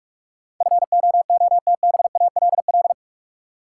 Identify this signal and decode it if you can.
Morse code, heard as a keyed tone, about 42 wpm, 700 Hz. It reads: FOOTBALL
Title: Sentences from Top 1600 Words - New Word Drill - 45wpm